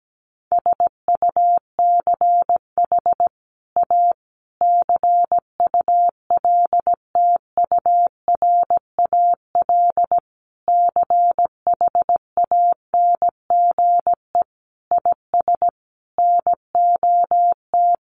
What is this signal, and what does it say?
Morse code, heard as a keyed tone, about 17 wpm, 700 Hz. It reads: SUCH A CULTURAL CHANGE IS NOT
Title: Sentences from Top 1500 Words 17wpm